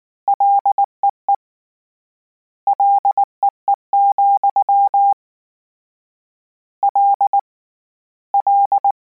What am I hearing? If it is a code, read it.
Morse code, heard as a keyed tone, about 19 wpm, 800 Hz. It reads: LEE LEE, L L
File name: Combinations / Combinations-mp3s / Combinations@18